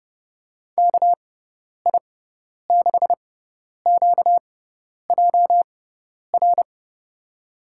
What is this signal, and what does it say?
Morse code, heard as a keyed tone, about 30 wpm, spaced as for 10 wpm, 700 Hz. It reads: KI6QJR
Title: USA Call Signs - 6 Characters - 6x Character Spacing - 30wpm